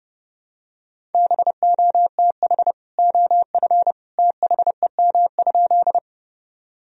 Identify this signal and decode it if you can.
Morse code, heard as a keyed tone, about 30 wpm, 700 Hz. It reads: BOTH OF THEM?
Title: Sentences from Top 1000 Words - Review Part 1 - 30wpm